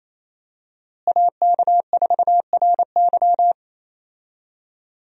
Morse code, heard as a keyed tone, about 27 wpm, 700 Hz. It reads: AK4RY
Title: USA Call Signs - 5 Characters - Repeated 3x - 28wpm